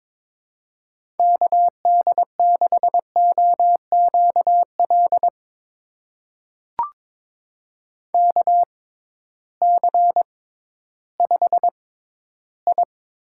Morse code, heard as a keyed tone, about 22 wpm, spaced as for 13 wpm, 700 Hz. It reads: KD6OQL KC5I
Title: USA Call Signs - 6 Characters - 6x Character Spacing - 22wpm